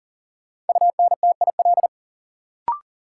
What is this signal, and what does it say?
Morse code, heard as a keyed tone, about 40 wpm, 700 Hz. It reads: UNTIL
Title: Sets of 5 Words Encore - New Word Drill - 40wpm